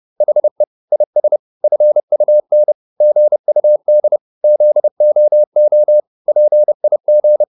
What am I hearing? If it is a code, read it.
Morse code, heard as a keyed tone, about 30 wpm, 600 Hz. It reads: HE IS FUN GUD ZOO PIG